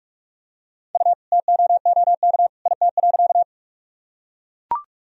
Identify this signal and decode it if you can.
Morse code, heard as a keyed tone, about 41 wpm, 700 Hz. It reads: U TOOK IT.
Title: Sentences from Top 300 Words and Abbreviations 45wpm